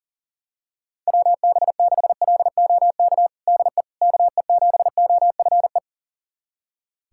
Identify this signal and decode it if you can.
Morse code, heard as a keyed tone, about 40 wpm, 700 Hz. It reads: WB6LOK DE KE7OFE
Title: USA Callsign DE Callsign - 6 Characters - 40wpm